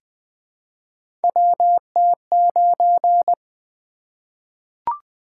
Morse code, heard as a keyed tone, about 20 wpm, 700 Hz. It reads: WT9
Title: Three Letter-Number Combinations - 20wpm